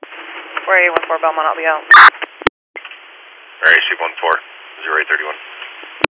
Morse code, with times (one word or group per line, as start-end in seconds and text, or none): none